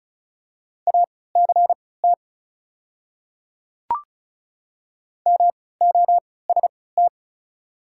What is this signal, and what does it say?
Morse code, heard as a keyed tone, about 34 wpm, spaced as for 20 wpm, 700 Hz. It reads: ACT MOST
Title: 3x Character Spacing - Top 300 Words - 35wpm